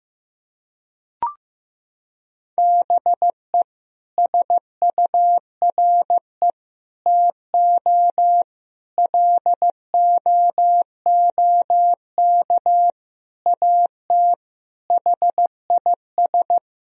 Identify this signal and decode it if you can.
Morse code, heard as a keyed tone, about 15 wpm, 700 Hz. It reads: BE SURE TO LOOK AT HIS
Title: Sentences from Top 1000 Words - Review Part 4 - 15wpm